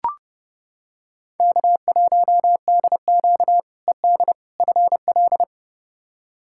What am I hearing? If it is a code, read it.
Morse code, heard as a keyed tone, about 30 wpm, 700 Hz. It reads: K1DQ ED FL
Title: K1USN SST QSO Practice - 30wpm